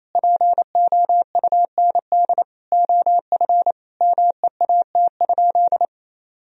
Morse code, heard as a keyed tone, about 28 wpm, 700 Hz. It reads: POUND OF MEAT?
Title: Sentences from Top 1000 Words - Review Part 3 - 28wpm